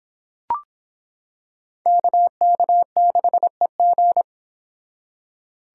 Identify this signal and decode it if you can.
Morse code, heard as a keyed tone, about 26 wpm, 700 Hz. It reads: KK6EG